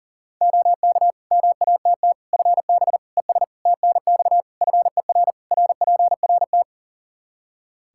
Morse code, heard as a keyed tone, about 40 wpm, 700 Hz. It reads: OK MATT FB ES TNX FER RPRT
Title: Practice QSO - Exchange 2 - Info - 40wpm